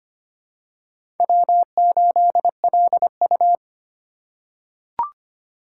Morse code, heard as a keyed tone, about 25 wpm, 700 Hz. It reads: W8LU
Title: USA Call Signs - 4 Characters - 6x Character Spacing - 25wpm